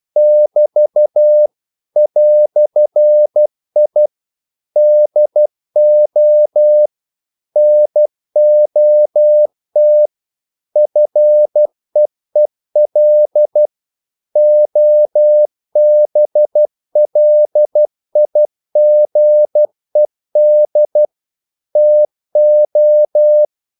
Morse code, heard as a keyed tone, about 12 wpm, 600 Hz. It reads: <BT> "I DO NOT FEEL OBLIGED TO
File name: CW / Practice / EnglishQuotations0112